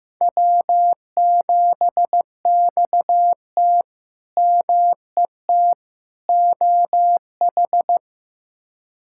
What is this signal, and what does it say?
Morse code, heard as a keyed tone, about 15 wpm, 700 Hz. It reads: W7XT MET OH
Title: K1USN SST QSO Practice - 15wpm